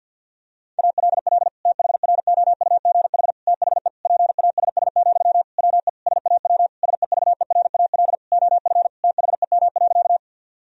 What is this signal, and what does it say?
Morse code, heard as a keyed tone, about 52 wpm, 700 Hz. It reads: ALL THROUGH THE PASS, WE SAW SEVERAL OF THEM.
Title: Sentences from Top 1000 Words - Review Part 2 - 50wpm